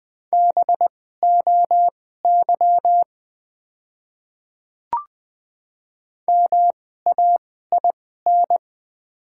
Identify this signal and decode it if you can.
Morse code, heard as a keyed tone, about 20 wpm, spaced as for 15 wpm, 690 Hz. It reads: BOY MAIN
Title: 2x Character Spacing - Top 300 Words - 20wpm